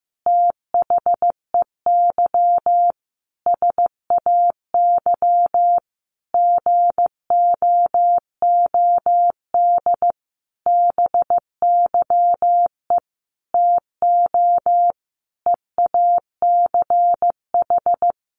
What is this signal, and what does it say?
Morse code, heard as a keyed tone, about 15 wpm, 700 Hz. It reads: THEY SAY GOOD BYE TO EACH
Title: Easy Phrases 15wpm